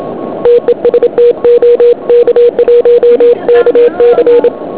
Morse code, heard as a keyed tone, about 27 wpm, 500 Hz. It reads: TEST OK1KC